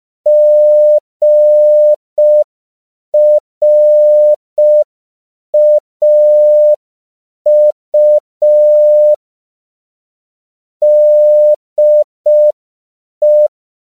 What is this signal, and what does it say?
Morse code, heard as a keyed tone, about 5 wpm, 600 Hz. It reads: GRAU DE
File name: CW / 5PPM